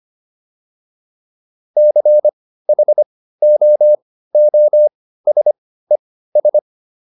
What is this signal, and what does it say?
Morse code, heard as a keyed tone, about 25 wpm, spaced as for 15 wpm, 600 Hz. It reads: CHOOSES